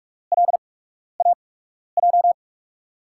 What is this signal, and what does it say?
Morse code, heard as a keyed tone, about 44 wpm, spaced as for 12 wpm, 700 Hz. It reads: LAJ
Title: USA Call Signs - 6 Characters - 8x Character Spacing - 45wpm